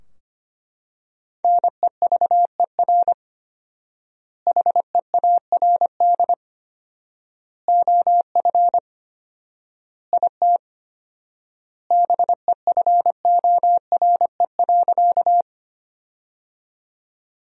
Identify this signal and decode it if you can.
Morse code, heard as a keyed tone, about 25 wpm, 700 Hz. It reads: NEVER HEARD OF IT BEFORE.
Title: 4x Word-Spacing - Sentences from Top 400 Words - 25wpm